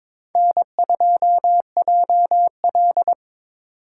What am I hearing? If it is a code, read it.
Morse code, heard as a keyed tone, about 26 wpm, 700 Hz. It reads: N2JL
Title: Speed Racing - USA Call Signs 22wpm